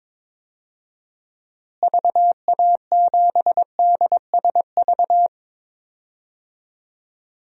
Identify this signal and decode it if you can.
Morse code, heard as a keyed tone, about 22 wpm, 700 Hz. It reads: VA7DSV
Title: DXCC-100 Call Signs - 6 Characters - Repeated 3x - 22wpm